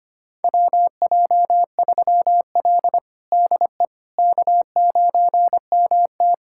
Morse code, heard as a keyed tone, about 25 wpm, 700 Hz. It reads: WJ3L DE K9MT
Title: USA Callsign DE Callsign - 4 Characters - 25wpm